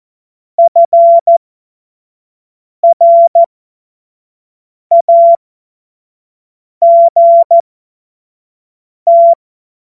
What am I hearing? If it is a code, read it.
Morse code, heard as a keyed tone, about 14 wpm, spaced as for 5 wpm, 680 Hz. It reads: FRAGT
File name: Morsekurs / Witze / Witz 16 Januar 2024 25 F70